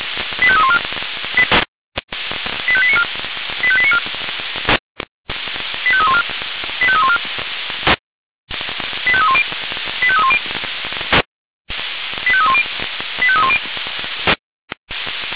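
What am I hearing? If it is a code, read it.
Five-tone selective call (ZVEI), heard as quick runs of five tones, tones of about 70 ms. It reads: 85314 85894 85894 85314 85314 85310 85310 85310 85310
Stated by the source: ZVEI five-tone selective call 85320 received by SDR.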